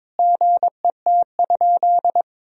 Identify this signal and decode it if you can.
Morse code, heard as a keyed tone, about 22 wpm, 700 Hz. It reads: GET?
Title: Sentences from Top 1000 Words - Review Part 3 - 22wpm